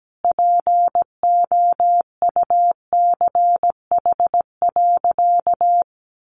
Morse code, heard as a keyed tone, about 17 wpm, 700 Hz. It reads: POUCH.
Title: Easy Phrases 17wpm